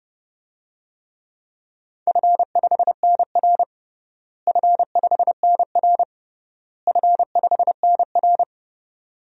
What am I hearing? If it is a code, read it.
Morse code, heard as a keyed tone, about 31 wpm, 700 Hz. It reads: F5NR F5NR F5NR
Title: DXCC-100 Call Signs - 4 Characters - Repeated 3x - 30wpm